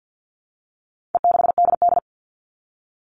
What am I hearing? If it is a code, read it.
Morse code, heard as a keyed tone, about 50 wpm, 700 Hz. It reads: E6DD
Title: All Cut Numbers - 4-Digit 50wpm